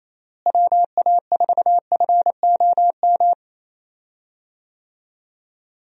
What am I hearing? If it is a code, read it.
Morse code, heard as a keyed tone, about 27 wpm, 700 Hz. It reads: WA4FOM
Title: USA Call Signs - 6 Characters - Repeated 3x - 28wpm